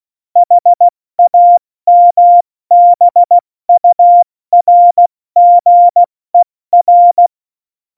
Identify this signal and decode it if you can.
Morse code, heard as a keyed tone, about 16 wpm, spaced as for 14 wpm, 700 Hz. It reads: HAMBURGER
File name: Morsekurs / Witze / Witz 05. Januar 2021, 70 F80